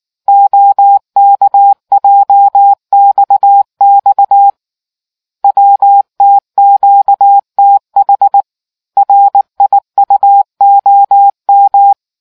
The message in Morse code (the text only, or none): OKJXX WTQTH RIUOM